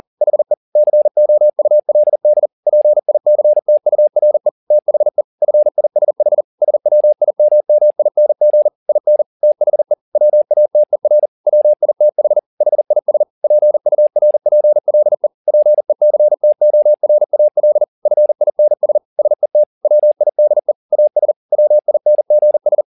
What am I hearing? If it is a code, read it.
Morse code, heard as a keyed tone, about 40 wpm, 600 Hz. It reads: HE COULD PICTURE THE FISH SWIMMING IN THE WATER WITH HIS PURPLE PECTORAL FINS SET WIDE AS WINGS